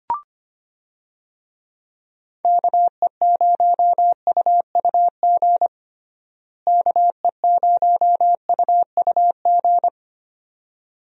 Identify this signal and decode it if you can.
Morse code, heard as a keyed tone, about 25 wpm, 700 Hz. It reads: KE0UUG KE0UUG
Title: USA Call Signs - 6 Characters - Repeated 2x - 25wpm